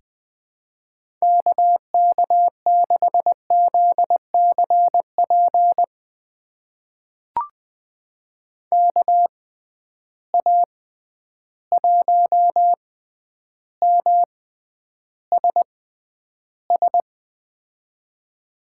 Morse code, heard as a keyed tone, about 20 wpm, spaced as for 10 wpm, 700 Hz. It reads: KK6ZCP KA1MSS
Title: USA Call Signs - 6 Characters - 6x Character Spacing - 20wpm